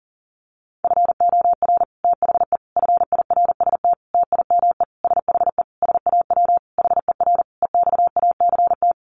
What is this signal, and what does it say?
Morse code, heard as a keyed tone, about 40 wpm, 700 Hz. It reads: FOR THE FIRST TIME SHE SAW HER EXACT
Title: Sentences from Top 1300 Words 40wpm